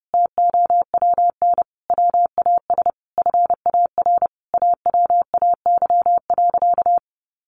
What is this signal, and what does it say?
Morse code, heard as a keyed tone, about 28 wpm, 700 Hz. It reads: TOWN WAS FAR AWAY.
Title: Sentences from Top 1500 Words 30wpm